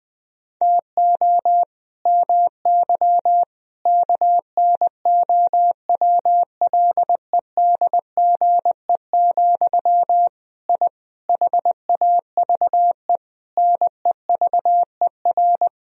Morse code, heard as a keyed tone, about 20 wpm, 700 Hz. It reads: TO MY KNOWLEDGE, I HAVE NEVER
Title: Sentences from Top 1000 Words - Review Part 3 - 20wpm